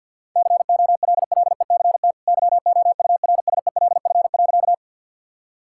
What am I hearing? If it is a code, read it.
Morse code, heard as a keyed tone, about 52 wpm, 700 Hz. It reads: COLLECT YOURSELF.
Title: Sentences from Top 1600 Words 50wpm